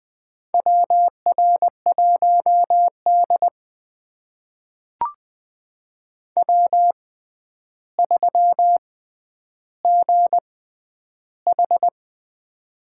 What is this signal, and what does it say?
Morse code, heard as a keyed tone, about 20 wpm, spaced as for 10 wpm, 700 Hz. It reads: WR1D W3GH
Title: USA Call Signs - 4 Characters - 6x Character Spacing - 20wpm